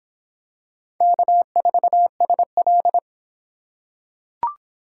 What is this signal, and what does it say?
Morse code, heard as a keyed tone, about 25 wpm, 700 Hz. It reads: K4SL